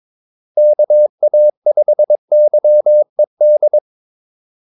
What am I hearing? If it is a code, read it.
Morse code, heard as a keyed tone, about 22 wpm, 600 Hz. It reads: KA5YED
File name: Callsigns-22wpm